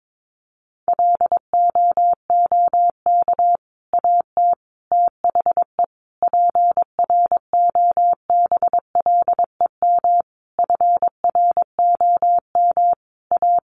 Morse code, heard as a keyed tone, about 22 wpm, 700 Hz. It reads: LOOK AT THE PROBLEM FROM A
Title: Sentences from Top 1500 Words 22wpm